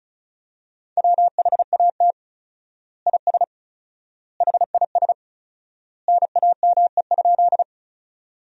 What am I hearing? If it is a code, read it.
Morse code, heard as a keyed tone, about 36 wpm, 700 Hz. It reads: WHAT IS HIS NAME?
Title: 4x Word-Spacing - Sentences from Top 200 Words - 35wpm